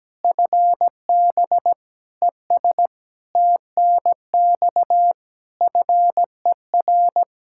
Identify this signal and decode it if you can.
Morse code, heard as a keyed tone, about 17 wpm, 700 Hz. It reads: FB ES TNX FER
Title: Practice QSO - Exchange 2 - Info - 17wpm